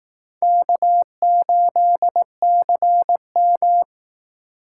Morse code, heard as a keyed tone, about 19 wpm, 700 Hz. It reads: K8CM